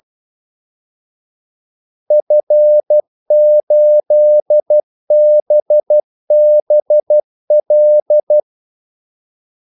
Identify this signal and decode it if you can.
Morse code, heard as a keyed tone, about 12 wpm, 600 Hz. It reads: F8BBL
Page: 2 ekstra ordmellemrum mellem kaldesignalerne